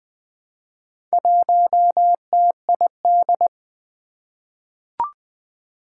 Morse code, heard as a keyed tone, about 20 wpm, 700 Hz. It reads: 1TID
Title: Four Challenging Letter-Number Combinations 20wpm